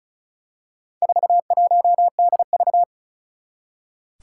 Morse code, heard as a keyed tone, about 36 wpm, 700 Hz. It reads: 41DV
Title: Four Challenging Letter-Number Combinations 35wpm